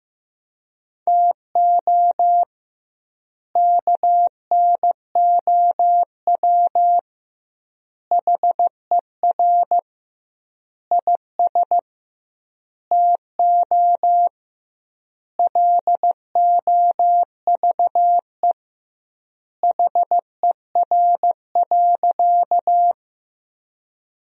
Morse code, heard as a keyed tone, about 15 wpm, 700 Hz. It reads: TO KNOW HER IS TO LOVE HER.
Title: 2x Word-Spacing - Sentences from Top 400 Words - 15wpm